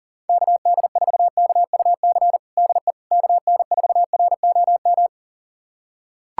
Morse code, heard as a keyed tone, about 40 wpm, 700 Hz. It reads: KD4KUC DE KN4ROM
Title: USA Callsign DE Callsign 40wpm